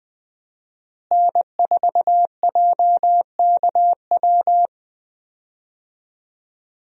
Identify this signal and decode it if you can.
Morse code, heard as a keyed tone, about 20 wpm, 700 Hz. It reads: N4JKW
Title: USA Call Signs - 5 Characters - Repeated 3x - 20wpm